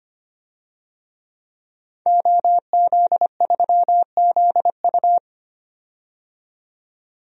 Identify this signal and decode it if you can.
Morse code, heard as a keyed tone, about 25 wpm, 700 Hz. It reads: OZ3ZU
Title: DXCC-100 Call Signs - 5 Characters - 25wpm